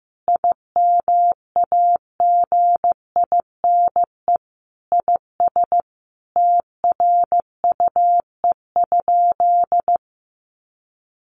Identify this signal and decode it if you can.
Morse code, heard as a keyed tone, about 15 wpm, 700 Hz. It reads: IMAGINE IS TRUE?
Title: Sentences from Top 1400 Words 15wpm